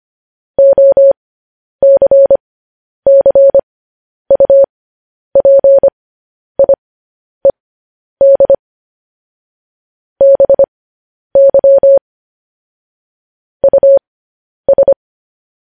Morse code, heard as a keyed tone, about 25 wpm, spaced as for 10 wpm, 550 Hz. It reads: OCCUPIED BY US